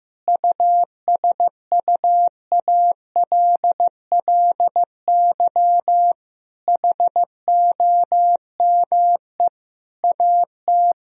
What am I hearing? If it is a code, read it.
Morse code, heard as a keyed tone, about 15 wpm, 700 Hz. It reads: USUALLY HOME AT
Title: Sentences from Top 1000 Words - Review Part 3 - 15wpm